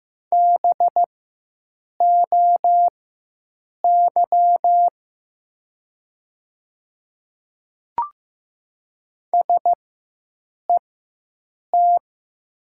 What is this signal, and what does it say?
Morse code, heard as a keyed tone, about 15 wpm, spaced as for 7 wpm, 700 Hz. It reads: BOY SET
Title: 4x Character Spacing - Top 200 Words - 15wpm